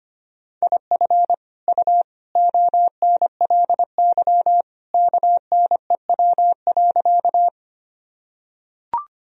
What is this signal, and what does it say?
Morse code, heard as a keyed tone, about 25 wpm, 700 Hz. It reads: IF U ONLY KNEW.
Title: Sentences from Top 400 Words and Abbreviations 25wpm